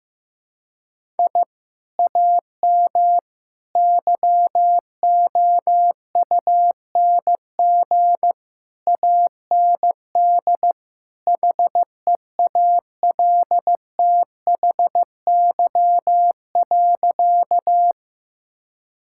Morse code, heard as a keyed tone, about 15 wpm, 700 Hz. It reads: I AM YOUNG AND HEALTHY.
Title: Sentences from Top 1000 Words - Review Part 5 - 15wpm